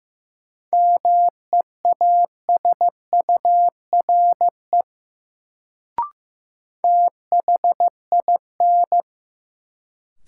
Morse code, heard as a keyed tone, about 15 wpm, 700 Hz. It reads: MEASURE THIN
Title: Sentences from Top 1200 Words - New Word Drill Alt - 15wpm